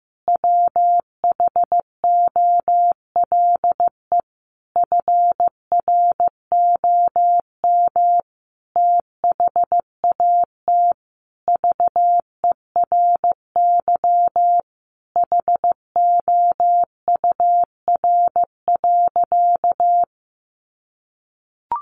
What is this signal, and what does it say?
Morse code, heard as a keyed tone, about 15 wpm, 700 Hz. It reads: WHOLE FROM THAT VERY HOUR.
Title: Sentences from Top 1500 Words 15wpm